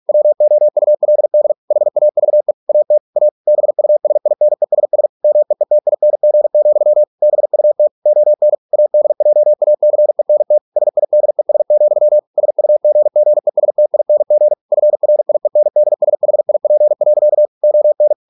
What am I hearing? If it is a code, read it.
Morse code, heard as a keyed tone, about 46 wpm, 600 Hz. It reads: WOULD HAVE AT A BUSINESS MEETING, BUT ON ADJACENT SIDES, SUGGESTING FRIENDSHIP. ON